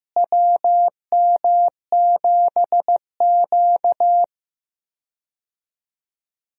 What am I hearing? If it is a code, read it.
Morse code, heard as a keyed tone, about 15 wpm, 700 Hz. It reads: WM7Q
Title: USA Call Signs - 4 Characters - Repeated 3x - 15wpm